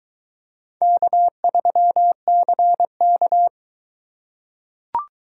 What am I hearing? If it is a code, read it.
Morse code, heard as a keyed tone, about 21 wpm, 700 Hz. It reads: K3CK